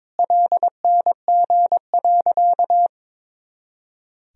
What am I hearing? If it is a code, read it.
Morse code, heard as a keyed tone, about 21 wpm, 700 Hz. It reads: LNG.
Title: Sentences from Top 100 Words and Abbreviations 22wpm